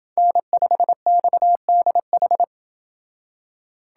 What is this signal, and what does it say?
Morse code, heard as a keyed tone, about 28 wpm, 700 Hz. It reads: N5XDH